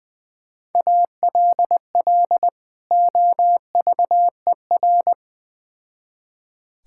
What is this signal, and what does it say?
Morse code, heard as a keyed tone, about 20 wpm, 700 Hz. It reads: ALL OVER
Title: Build A Phrase - 3-Word Phrases - 20wpm